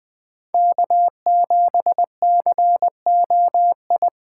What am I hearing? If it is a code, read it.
Morse code, heard as a keyed tone, about 20 wpm, 700 Hz. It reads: K7COI
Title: USA Call Signs - 5 Characters - 3x Character Spacing - 20wpm